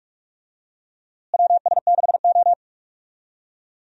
Morse code, heard as a keyed tone, about 46 wpm, 700 Hz. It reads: WS6O
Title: USA Call Signs - 4 Characters - Repeated 3x - 45wpm